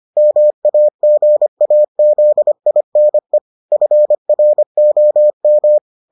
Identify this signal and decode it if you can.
Morse code, heard as a keyed tone, about 25 wpm, 600 Hz. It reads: MAGAZINE FROM